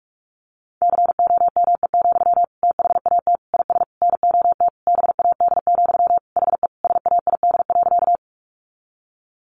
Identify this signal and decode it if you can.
Morse code, heard as a keyed tone, about 44 wpm, 700 Hz. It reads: COME, THAT IS NOT BAD, HE SAID.
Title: Sentences from Top 700 Words 45wpm